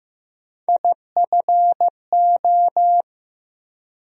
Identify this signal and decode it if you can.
Morse code, heard as a keyed tone, about 15 wpm, 700 Hz. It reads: IFO
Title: USA Call Sign Postfixes - 3 Letters 15wpm